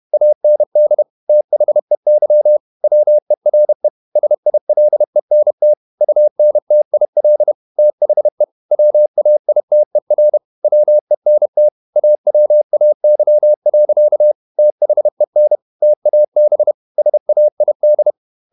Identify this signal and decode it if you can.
Morse code, heard as a keyed tone, about 31 wpm, 600 Hz. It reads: AND THEY WERE SILENT UNTIL THE WAITER WENT AWAY. THEN TAB SAID